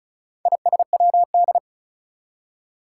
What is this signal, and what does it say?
Morse code, heard as a keyed tone, about 36 wpm, 700 Hz. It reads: ISWD
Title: Four Challenging Letter-Number Combinations 35wpm